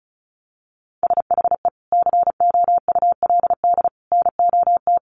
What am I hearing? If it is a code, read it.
Morse code, heard as a keyed tone, about 34 wpm, 700 Hz. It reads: SHE COULD NOT
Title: Sentences from Top 1000 Words 35wpm